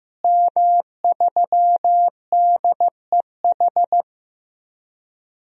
Morse code, heard as a keyed tone, about 15 wpm, 700 Hz. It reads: M3DEH
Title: DXCC-100 Call Signs - 5 Characters - Repeated 2x - 15wpm